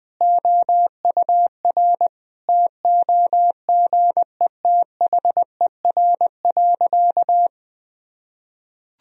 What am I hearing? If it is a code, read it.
Morse code, heard as a keyed tone, about 19 wpm, 700 Hz. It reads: OUR TOGETHER.
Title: Sentences from Top 1000 Words - Review Part 1 - 20wpm